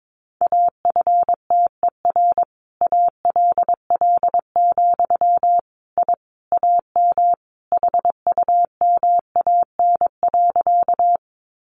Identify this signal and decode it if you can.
Morse code, heard as a keyed tone, about 22 wpm, 700 Hz. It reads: AFTER ALL, I AM HUMAN.
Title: Sentences from Top 600 Words 22wpm